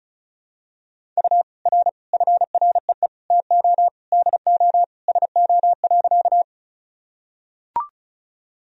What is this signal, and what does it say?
Morse code, heard as a keyed tone, about 34 wpm, 700 Hz. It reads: U R FREE TO DO SO.
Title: Sentences from Top 500 Words and Abbreviations 35wpm